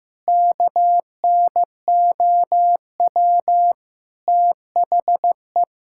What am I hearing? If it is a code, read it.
Morse code, heard as a keyed tone, about 16 wpm, 700 Hz. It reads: KNOW THE
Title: Sentences from Top 1000 Words - Review Part 6 - 15wpm